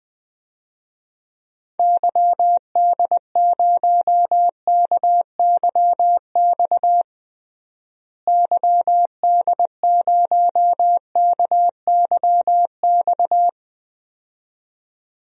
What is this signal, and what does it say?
Morse code, heard as a keyed tone, about 20 wpm, 700 Hz. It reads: YD0KYX YD0KYX
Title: DXCC-100 Call Signs - 6 Characters - Repeated 2x - 20wpm